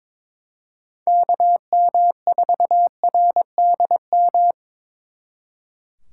Morse code, heard as a keyed tone, about 25 wpm, 700 Hz. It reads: KM4RDM
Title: Speed Racing - USA 2x3 Call Signs 22wpm